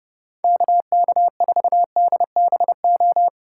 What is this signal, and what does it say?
Morse code, heard as a keyed tone, about 37 wpm, 700 Hz. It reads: KK4DBO
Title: Speed Racing - USA 2x3 Call Signs 30wpm